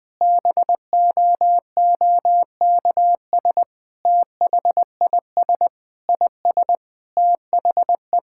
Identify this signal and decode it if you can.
Morse code, heard as a keyed tone, about 21 wpm, 700 Hz. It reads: BOOKS THIS IS THE
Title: Sentences with 3-Word Phrases - 20wpm